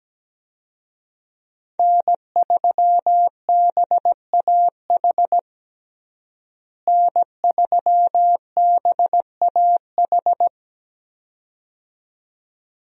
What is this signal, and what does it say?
Morse code, heard as a keyed tone, about 17 wpm, 700 Hz. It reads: N3BAH N3BAH
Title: USA Call Signs - 5 Characters - Repeated 2x - 17wpm